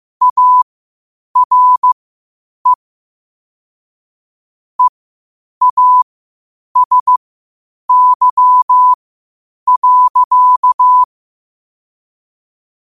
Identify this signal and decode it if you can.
Morse code, heard as a keyed tone, about 15 wpm, spaced as for 8 wpm, 1000 Hz. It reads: ARE EASY.
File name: qotdfiles / 8wpm